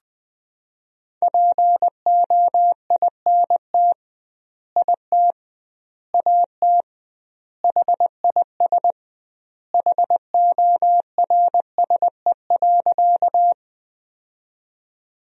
Morse code, heard as a keyed tone, about 20 wpm, 700 Hz. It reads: POINT IT AT HIS HORSE.
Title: 2x Word-Spacing - Sentences from Top 300 Words - 20wpm